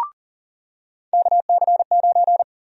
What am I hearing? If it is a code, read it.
Morse code, heard as a keyed tone, about 40 wpm, 700 Hz. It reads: KC9
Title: USA Call Sign Prefixes - 40wpm